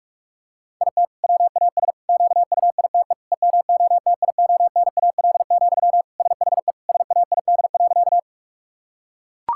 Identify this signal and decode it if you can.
Morse code, heard as a keyed tone, about 41 wpm, 700 Hz. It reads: IT WAS QUITE EMOTIONAL, SHE SAID.
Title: Sentences from Top 1000 Words - Review Part 5 - 45wpm